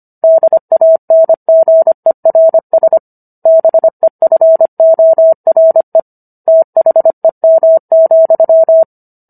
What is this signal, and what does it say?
Morse code, heard as a keyed tone, about 25 wpm, 650 Hz. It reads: DANGERS BEFORE THEM,
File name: Book One / 16-Chapter 16 - 25 WPM